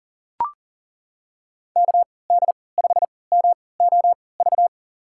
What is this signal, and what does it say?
Morse code, heard as a keyed tone, about 40 wpm, spaced as for 23 wpm, 700 Hz. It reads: KD5MOV
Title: USA Call Signs - 6 Characters - 3x Character Spacing - 40wpm